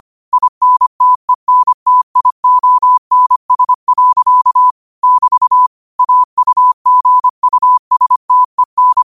Morse code, heard as a keyed tone, about 25 wpm, 1000 Hz. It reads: INTENTIONS. <BT> AUGUSTEN
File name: qotdfiles / 25wpm